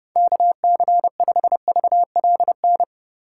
Morse code, heard as a keyed tone, about 32 wpm, 700 Hz. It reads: KC5VLN
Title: USA Call Signs - 6 Characters - 2x Character Spacing - 30wpm